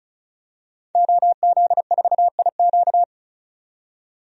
Morse code, heard as a keyed tone, about 34 wpm, 700 Hz. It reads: OZ4IQ
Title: DXCC-100 Call Signs - 5 Characters - 35wpm